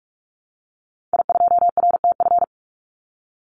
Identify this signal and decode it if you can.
Morse code, heard as a keyed tone, about 44 wpm, 700 Hz. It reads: I2RTF
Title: Call Signs - 45wpm